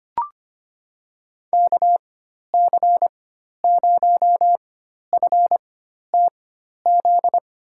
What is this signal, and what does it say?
Morse code, heard as a keyed tone, about 25 wpm, spaced as for 12 wpm, 700 Hz. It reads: KC0FTZ
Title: USA Call Signs - 6 Characters - 4x Character Spacing - 25wpm